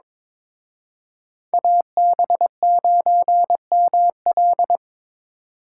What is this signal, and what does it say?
Morse code, heard as a keyed tone, about 26 wpm, 700 Hz. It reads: AB9ML
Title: Speed Racing - USA Call Signs 22wpm